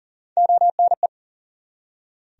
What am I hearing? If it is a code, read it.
Morse code, heard as a keyed tone, about 40 wpm, 700 Hz. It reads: ONE
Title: Top 100 Words 40wpm